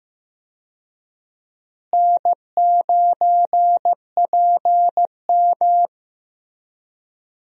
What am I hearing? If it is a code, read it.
Morse code, heard as a keyed tone, about 16 wpm, 700 Hz. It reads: N9PM
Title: USA Call Signs - 4 Characters - Repeated 2x - 15wpm